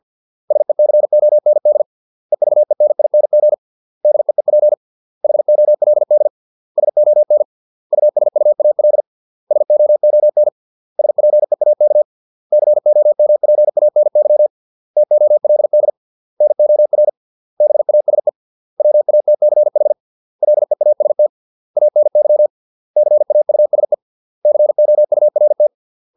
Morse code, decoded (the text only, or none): SECOND EVENING DEEP HOLD SON USUAL SOON SPEAK COMPANY TOLD NOR BASE WATCH LEAST ANY CAUSE COURT